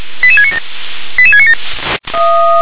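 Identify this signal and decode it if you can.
Five-tone selective call (ZVEI), heard as quick runs of five tones, tones of about 70 ms. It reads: 88697 88697
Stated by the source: ZVEI five-tone selective call 88783 received with an SDR.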